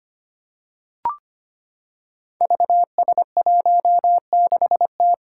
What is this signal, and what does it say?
Morse code, heard as a keyed tone, about 25 wpm, 700 Hz. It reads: VS16T
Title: Five Challenging Letter-Number Combinations 25wpm